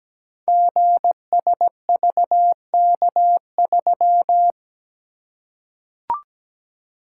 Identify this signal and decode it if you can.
Morse code, heard as a keyed tone, about 17 wpm, 700 Hz. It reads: GSVK3
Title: Code Groups - 5 Letter:Numbers - 17wpm